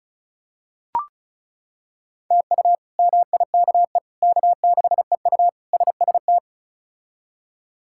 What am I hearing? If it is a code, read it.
Morse code, heard as a keyed tone, about 36 wpm, 700 Hz. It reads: TU MIKE K6EU SST
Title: K1USN SST QSO Practice - 35wpm